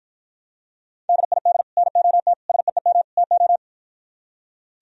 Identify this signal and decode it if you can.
Morse code, heard as a keyed tone, about 52 wpm, 700 Hz. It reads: DID NOT SEEM TO